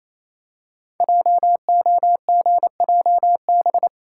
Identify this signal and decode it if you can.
Morse code, heard as a keyed tone, about 27 wpm, 700 Hz. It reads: JOGJB
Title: Code Groups - 5 Letters - 28wpm